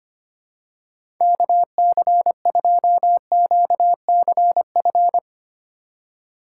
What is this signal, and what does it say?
Morse code, heard as a keyed tone, about 25 wpm, 700 Hz. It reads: KC2QCF
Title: USA Call Signs - 6 Characters - 3x Character Spacing - 25wpm